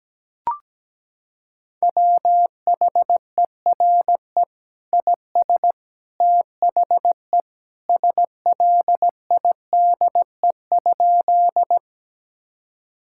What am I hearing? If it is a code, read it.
Morse code, heard as a keyed tone, about 17 wpm, 700 Hz. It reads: WHERE IS THE SLIDE?
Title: Sentences from Top 1600 Words 17wpm